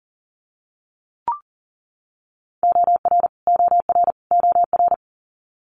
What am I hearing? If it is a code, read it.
Morse code, heard as a keyed tone, about 37 wpm, 700 Hz. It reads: OR OR OR
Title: Multi-letter Phonemes Repeated 3x 40wpm